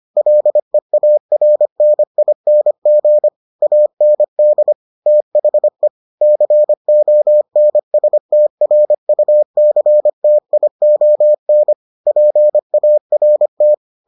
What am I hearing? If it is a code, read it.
Morse code, heard as a keyed tone, about 25 wpm, 600 Hz. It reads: LEARNING AND THE CONSTRUCTION PART